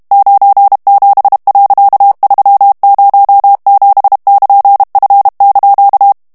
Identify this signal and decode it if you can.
Morse code, heard as a keyed tone, about 32 wpm, 800 Hz. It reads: 97.307<KN>F)
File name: morse mp3 / ANGLAIS texte clair 3 vitesse 30 mots minute